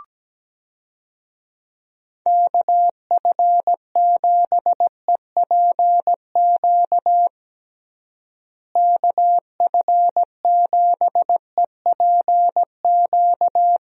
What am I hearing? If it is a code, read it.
Morse code, heard as a keyed tone, about 17 wpm, 700 Hz. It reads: KF7EPQ KF7EPQ
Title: USA Call Signs - 6 Characters - Repeated 2x - 17wpm